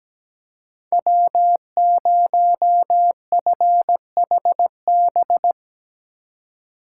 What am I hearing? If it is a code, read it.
Morse code, heard as a keyed tone, about 17 wpm, 700 Hz. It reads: W0FHB
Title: USA Call Signs - 5 Characters - 2x Character Spacing - 17wpm